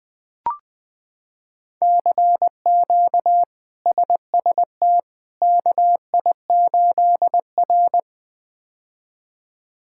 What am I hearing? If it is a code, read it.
Morse code, heard as a keyed tone, about 20 wpm, 700 Hz. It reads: CQ SST KI8R
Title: K1USN SST QSO Practice - 20wpm